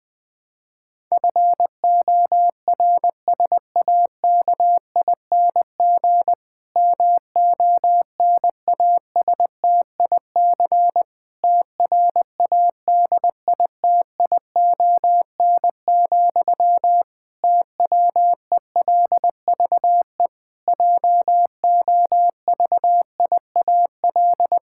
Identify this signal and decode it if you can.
Morse code, heard as a keyed tone, about 21 wpm, 700 Hz. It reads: FORSAKING MONASTIC TRADITION, TWELVE JOVIAL
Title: Pangram Challenge v2 - 20wpm